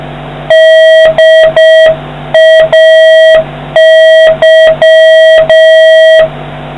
Morse code, heard as a keyed tone, about 6 wpm, 1900 Hz, with noise. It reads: DAY